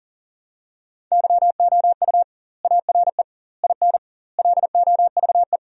Morse code, heard as a keyed tone, about 40 wpm, 700 Hz. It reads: YOU ARE IN LOVE
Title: 2x Word-Spacing - Sentences from Top 400 Words - 40wpm